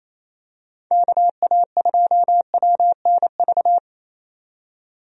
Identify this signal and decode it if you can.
Morse code, heard as a keyed tone, about 27 wpm, 700 Hz. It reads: KA2WNV
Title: USA Call Signs - 6 Characters - 28wpm